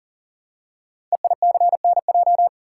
Morse code, heard as a keyed tone, about 40 wpm, 700 Hz. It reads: EICNJ
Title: Code Groups - 5 Letters - 40wpm